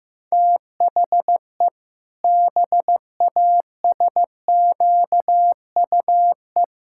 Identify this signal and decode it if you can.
Morse code, heard as a keyed tone, about 14 wpm, 700 Hz. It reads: THE BASQUE
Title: Pangram Challenge v2 - 15wpm